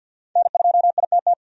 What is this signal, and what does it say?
Morse code, heard as a keyed tone, about 53 wpm, 700 Hz. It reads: N2ITT
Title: USA Call Signs - 5 Characters - 50wpm